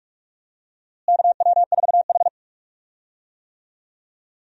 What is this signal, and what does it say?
Morse code, heard as a keyed tone, about 45 wpm, 700 Hz. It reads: KW4H
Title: USA Call Signs - 4 Characters - Repeated 3x - 45wpm